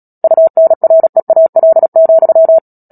Morse code, heard as a keyed tone, about 37 wpm, 650 Hz. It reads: UNREAL,
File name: Book Two / 10-Chapter 10 - Epilogue - 36 WPM